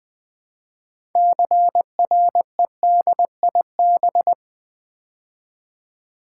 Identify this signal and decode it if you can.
Morse code, heard as a keyed tone, about 20 wpm, 700 Hz. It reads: CREDIB
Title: Build A Word - 2 Letters At A Time - 8-Letter Words - 20wpm